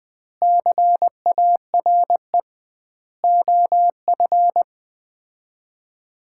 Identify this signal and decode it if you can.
Morse code, heard as a keyed tone, about 20 wpm, 700 Hz. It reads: CARE OF
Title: 2x Word-Spacing - Sets of 4 Words - 20wpm